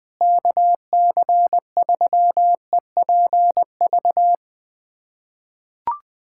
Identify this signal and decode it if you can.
Morse code, heard as a keyed tone, about 21 wpm, 700 Hz. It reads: KC3EPV
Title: USA Callsign DE Callsign - 6 Characters - 20wpm